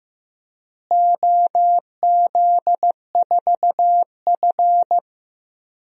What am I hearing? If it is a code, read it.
Morse code, heard as a keyed tone, about 15 wpm, 700 Hz. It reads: OZ4F
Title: DXCC-100 Call Signs - 4 Characters - Repeated 3x - 15wpm